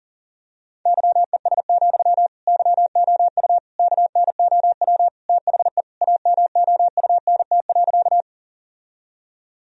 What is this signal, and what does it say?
Morse code, heard as a keyed tone, about 40 wpm, 700 Hz. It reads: YES, YOU KNOW THE AMOUNT.
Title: Sentences from Top 1000 Words - Review Part 4 - 40wpm